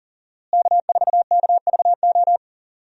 Morse code, heard as a keyed tone, about 40 wpm, 700 Hz. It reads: K4KVO
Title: USA Call Signs - 5 Characters - Repeated 3x - 40wpm